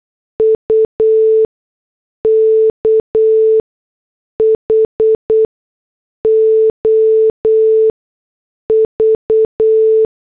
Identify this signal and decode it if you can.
Morse code, heard as a keyed tone, about 8 wpm, spaced as for 6 wpm, 440 Hz. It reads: UKHOV